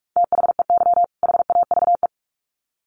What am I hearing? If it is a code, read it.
Morse code, heard as a keyed tone, about 44 wpm, 700 Hz. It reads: THEY HAVE
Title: Sets of 2 Words 45wpm